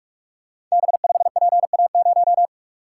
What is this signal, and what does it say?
Morse code, heard as a keyed tone, about 44 wpm, 700 Hz. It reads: B5PA0
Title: Code Groups - 5 Letter:Numbers - 45wpm